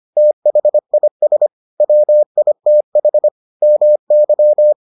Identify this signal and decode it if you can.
Morse code, heard as a keyed tone, about 25 wpm, 600 Hz. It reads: THIS WITH MY